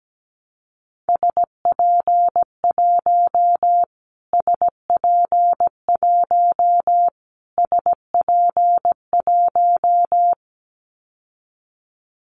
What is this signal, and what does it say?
Morse code, heard as a keyed tone, about 17 wpm, 700 Hz. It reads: SP1 SP1 SP1
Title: Callsign Prefixes for Top 25 DXCC - Repeated 3x - 17wpm